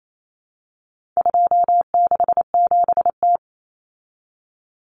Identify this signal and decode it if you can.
Morse code, heard as a keyed tone, about 28 wpm, 700 Hz. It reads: 267T
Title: Common Cut Numbers - 4-Digit 28wpm